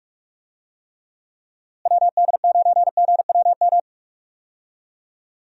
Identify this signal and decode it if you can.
Morse code, heard as a keyed tone, about 43 wpm, 700 Hz. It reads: WD9GWM
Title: USA Call Signs - 6 Characters - Repeated 2x - 45wpm